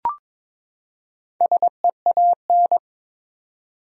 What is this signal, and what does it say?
Morse code, heard as a keyed tone, about 22 wpm, 700 Hz. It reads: SEAN
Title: USA Common Names - Max 4 Letters - 22wpm